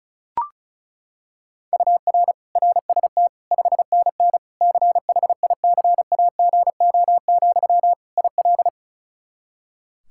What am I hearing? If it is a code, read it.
Morse code, heard as a keyed tone, about 36 wpm, 700 Hz. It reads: UR RST 5NN CHICAGO, IL
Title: K1USN SST QSO Practice 35wpm